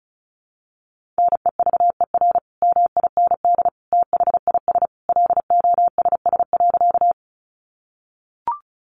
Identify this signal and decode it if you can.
Morse code, heard as a keyed tone, about 34 wpm, 700 Hz. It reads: NEVER MIND THIS LOSS.
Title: Sentences from Top 1400 Words 35wpm